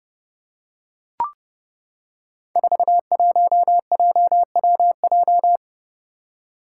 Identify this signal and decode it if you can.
Morse code, heard as a keyed tone, about 32 wpm, 700 Hz. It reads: 41JWJ
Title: Five Challenging Letter-Number Combinations 30wpm